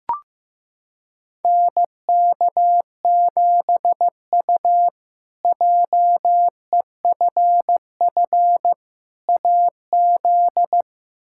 Morse code, heard as a keyed tone, about 15 wpm, 700 Hz. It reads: NK7U JEFF AZ
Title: K1USN SST QSO Practice - 15wpm